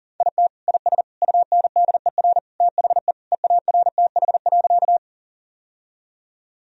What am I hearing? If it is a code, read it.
Morse code, heard as a keyed tone, about 40 wpm, 700 Hz. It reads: IT IS UNDER THE EARTH.
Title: Sentences from Top 1000 Words - Review Part 1 - 40wpm